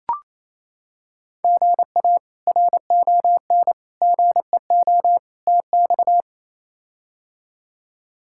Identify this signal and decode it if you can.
Morse code, heard as a keyed tone, about 28 wpm, 700 Hz. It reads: GA RON GEO TX
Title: K1USN SST QSO Practice - 28wpm